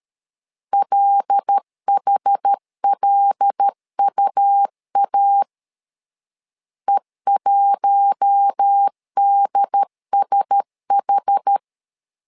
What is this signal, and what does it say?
Morse code, heard as a keyed tone, about 13 wpm, 800 Hz. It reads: LHLUA E1DSH
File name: LEZIONI / 5 LBVC / 5 L 12wpm 1913